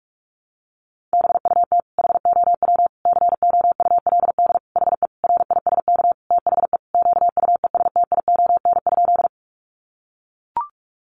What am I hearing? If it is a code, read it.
Morse code, heard as a keyed tone, about 44 wpm, 700 Hz. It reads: BUT HOW COULD HE RISK THE QUESTION?
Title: Sentences from Top 600 Words 45wpm